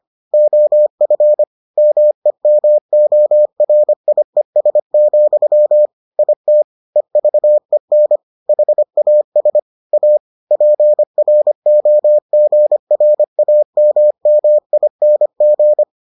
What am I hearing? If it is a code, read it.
Morse code, heard as a keyed tone, about 25 wpm, 600 Hz. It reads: OF MEMORIES, IT EVEN HAS A PROGRAMMING